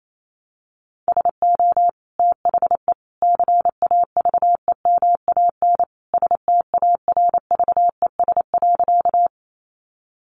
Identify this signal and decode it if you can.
Morse code, heard as a keyed tone, about 28 wpm, 700 Hz. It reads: SO THE CAVEMAN STARVES.
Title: A Sound Of Thunder 28wpm